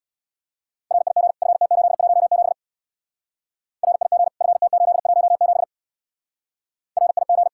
Morse code, heard as a keyed tone, about 66 wpm, 700 Hz. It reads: RIG FT817 RIG FT817 RIG